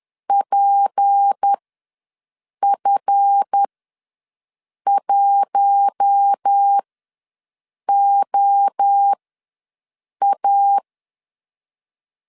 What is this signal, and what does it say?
Morse code, heard as a keyed tone, about 10 wpm, spaced as for 6 wpm, 800 Hz. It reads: PF1OA
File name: LEZIONI / 7 GF9Z / 7 GF ws 10wpm 3918